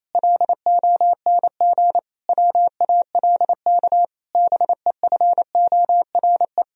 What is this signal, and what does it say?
Morse code, heard as a keyed tone, about 28 wpm, 700 Hz. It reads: LONG WALK BEFORE
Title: Sentences from Top 1000 Words - Review Part 1 - 28wpm